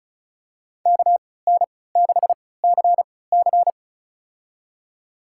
Morse code, heard as a keyed tone, about 34 wpm, spaced as for 20 wpm, 700 Hz. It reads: KN6CC
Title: USA Call Signs - 5 Characters - 3x Character Spacing - 35wpm